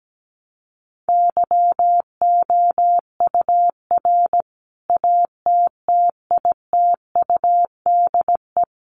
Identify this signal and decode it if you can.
Morse code, heard as a keyed tone, about 17 wpm, 700 Hz. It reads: YOUR ATTITUDE
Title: Sentences from Top 1300 Words 17wpm